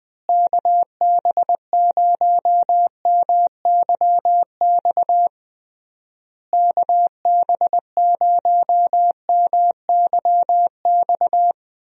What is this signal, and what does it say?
Morse code, heard as a keyed tone, about 21 wpm, 700 Hz. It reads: KB0MYX KB0MYX
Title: USA Call Signs - 6 Characters - Repeated 3x - 20wpm